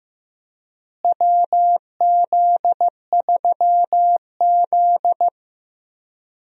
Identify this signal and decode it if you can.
Morse code, heard as a keyed tone, about 15 wpm, 700 Hz. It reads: WZ3Z
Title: USA Call Signs - 4 Characters - Repeated 2x - 15wpm